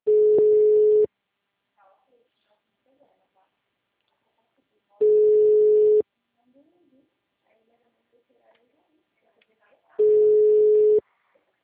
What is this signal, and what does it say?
Ringback tone (Europe, 425 Hz), with speech.